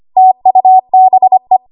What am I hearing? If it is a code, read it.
Morse code, heard as a keyed tone, about 25 wpm, 750 Hz. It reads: TUBE
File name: Archive / 25 WPM / 25WPM